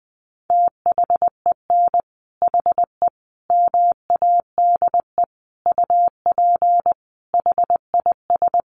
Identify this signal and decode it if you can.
Morse code, heard as a keyed tone, about 20 wpm, 700 Hz. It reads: THEN HE MADE UP HIS
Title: Sentences from Top 1400 Words 20wpm